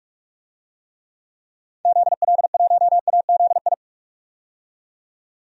Morse code, heard as a keyed tone, about 41 wpm, 690 Hz. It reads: ZL1AZI
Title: DXCC-100 Call Signs - 6 Characters - 45wpm